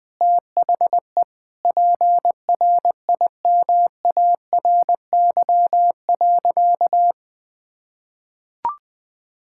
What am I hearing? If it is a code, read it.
Morse code, heard as a keyed tone, about 20 wpm, 700 Hz. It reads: THE PRIMARY.
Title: Sentences from Top 1000 Words - Review Part 5 - 20wpm